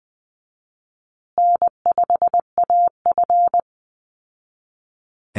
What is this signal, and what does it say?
Morse code, heard as a keyed tone, about 20 wpm, 700 Hz. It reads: N5AF
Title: Easy Call Signs Repeated 3x - 20wpm